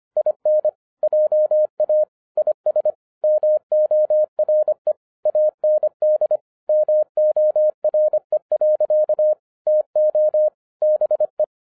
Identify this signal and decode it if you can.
Morse code, heard as a keyed tone, about 25 wpm, 610 Hz. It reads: IN JA IS MORE AND MORE. TO BE
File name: 2012-01-mp3 / n08-ITS MY FAULT, BUT-25wpm